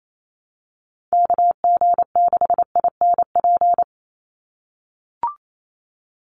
Morse code, heard as a keyed tone, about 28 wpm, 700 Hz. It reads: KG6INP
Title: USA Call Signs 28wpm